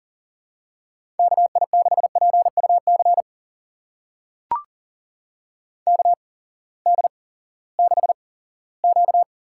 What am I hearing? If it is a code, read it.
Morse code, heard as a keyed tone, about 39 wpm, spaced as for 20 wpm, 700 Hz. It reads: KI6PUC KD6Q
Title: USA Call Signs - 6 Characters - 8x Character Spacing - 40wpm